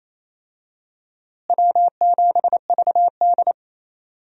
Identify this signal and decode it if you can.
Morse code, heard as a keyed tone, about 28 wpm, 700 Hz. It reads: W7VD
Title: USA Call Signs - 4 Characters - Repeated 2x - 28wpm